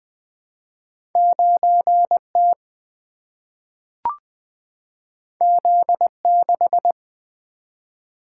Morse code, heard as a keyed tone, about 20 wpm, 700 Hz. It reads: 9 T Z 6
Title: Single-Letter Single-Number 20wpm